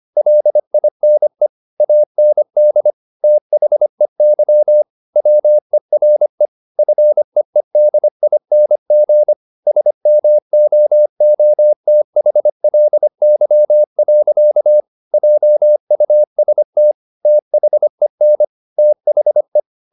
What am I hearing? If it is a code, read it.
Morse code, heard as a keyed tone, about 25 wpm, 600 Hz. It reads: LINE AND THEY WERE FEEDING SMOOTHLY. JUST THEN THE